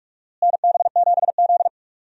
Morse code, heard as a keyed tone, about 46 wpm, 700 Hz. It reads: NB7Z
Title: USA Call Signs - 4 Characters - Repeated 3x - 45wpm